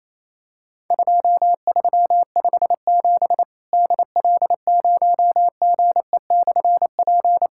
Morse code, heard as a keyed tone, about 28 wpm, 700 Hz. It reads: 2357 DL0GE/P
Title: Worked All Europe DX Contest - QTC Exchange - 28wpm